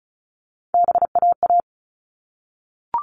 Morse code, heard as a keyed tone, about 34 wpm, 700 Hz. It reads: BAA
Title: All Cut Numbers - 3-Digit 35wpm